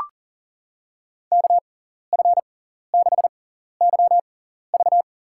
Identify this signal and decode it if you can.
Morse code, heard as a keyed tone, about 37 wpm, spaced as for 13 wpm, 700 Hz. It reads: KF6YV
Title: USA Call Signs - 5 Characters - 6x Character Spacing - 40wpm